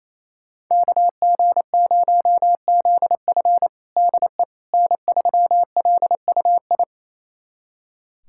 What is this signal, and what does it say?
Morse code, heard as a keyed tone, about 28 wpm, 700 Hz. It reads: KG0ZF DE N3LUI
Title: USA Callsign DE Callsign - 5 Characters - 28wpm